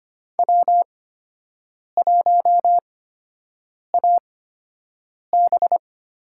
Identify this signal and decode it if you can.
Morse code, heard as a keyed tone, about 25 wpm, spaced as for 7 wpm, 700 Hz. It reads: W1AB
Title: USA Call Signs - 8x Character Spacing - 25wpm